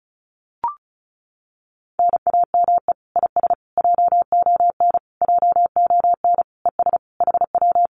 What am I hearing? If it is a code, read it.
Morse code, heard as a keyed tone, about 36 wpm, 700 Hz. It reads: NAME IS JON JON ES HW
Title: Common QSO Phrases Encore 35wpm